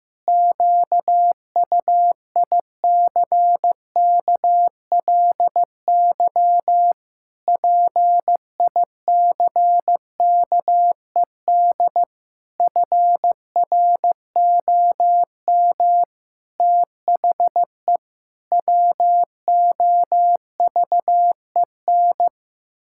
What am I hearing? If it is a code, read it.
Morse code, heard as a keyed tone, about 16 wpm, 700 Hz. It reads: QUICKLY PICKED FROM THE WOVEN
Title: Pangram Challenge v2 - 15wpm